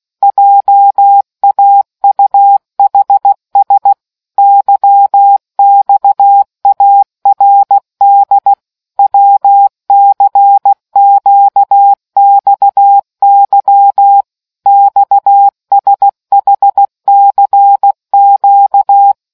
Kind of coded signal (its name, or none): Morse code